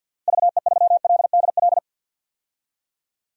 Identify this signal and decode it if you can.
Morse code, heard as a keyed tone, about 52 wpm, 700 Hz. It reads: VE3LDL
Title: DXCC-100 Call Signs - 6 Characters - 50wpm